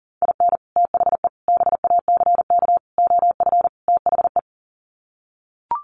Morse code, heard as a keyed tone, about 40 wpm, 700 Hz. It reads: IN THE BACK OF THE
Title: Sets of 5 Words 40wpm